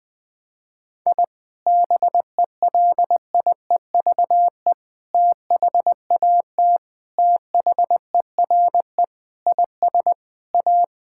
Morse code, heard as a keyed tone, about 20 wpm, 700 Hz. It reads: I BELIEVE THAT THERE IS A
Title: Sentences with 4-Word Phrases - 20wpm